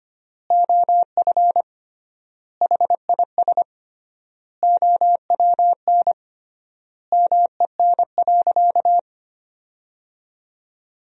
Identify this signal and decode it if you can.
Morse code, heard as a keyed tone, about 25 wpm, 700 Hz. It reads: OF HIS OWN MEN.
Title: 3x Word-Spacing - Sentences from Top 300 Words - 25wpm